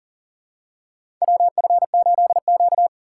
Morse code, heard as a keyed tone, about 40 wpm, 700 Hz. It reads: WF8Q
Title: USA Call Signs - 4 Characters - 2x Character Spacing - 40wpm